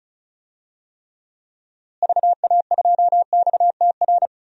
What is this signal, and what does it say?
Morse code, heard as a keyed tone, about 36 wpm, 700 Hz. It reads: VA2XTR
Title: DXCC-100 Call Signs - 6 Characters - Repeated 3x - 35wpm